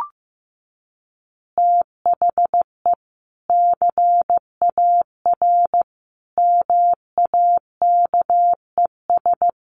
Morse code, heard as a keyed tone, about 15 wpm, 700 Hz. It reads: THE CAR MAKES
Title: Easy Phrases 15wpm